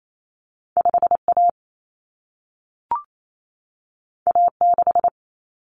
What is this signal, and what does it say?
Morse code, heard as a keyed tone, about 28 wpm, 700 Hz. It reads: 5A A6
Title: Common Cut Numbers - 2-Digit 28wpm